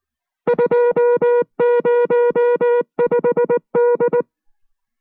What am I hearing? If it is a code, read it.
Morse code, heard as a keyed tone, about 19 wpm, 480 Hz. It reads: 205D